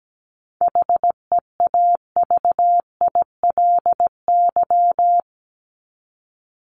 Morse code, heard as a keyed tone, about 17 wpm, 700 Hz. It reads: HEAVILY
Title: LY Suffix - Top 100 Words with Base Words 17wpm